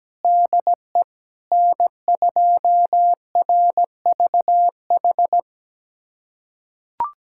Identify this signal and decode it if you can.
Morse code, heard as a keyed tone, about 17 wpm, 700 Hz. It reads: DE N2RVH
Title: USA Callsign DE Callsign - 5 Characters - 17wpm